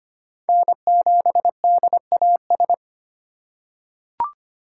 Morse code, heard as a keyed tone, about 25 wpm, 700 Hz. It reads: N7DAS
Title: USA Call Signs - 5 Characters - 8x Character Spacing - 25wpm